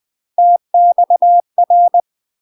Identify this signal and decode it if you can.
Morse code, heard as a keyed tone, about 20 wpm, 700 Hz. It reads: TXR
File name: UK MASTER QSOs TRAINING / 20WPM / UK MASTER QSOs 0065